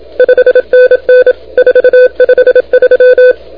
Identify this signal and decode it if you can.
Morse code, heard as a keyed tone, about 27 wpm, 500 Hz. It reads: 5NN 453
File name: recordings / HH2-N5JR de NT1K ARRL10M CW 28MHz